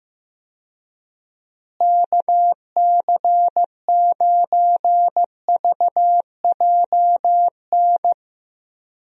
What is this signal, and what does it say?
Morse code, heard as a keyed tone, about 15 wpm, 700 Hz. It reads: KC9VJN
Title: USA Call Signs - 6 Characters - Repeated 2x - 15wpm